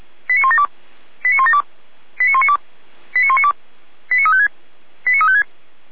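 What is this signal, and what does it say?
Five-tone selective call (ZVEI), heard as quick runs of five tones, tones of about 70 ms. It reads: 89172 89172 89192 89192 89356 89356
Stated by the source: ZVEI five-tone call 89350 received on SDR.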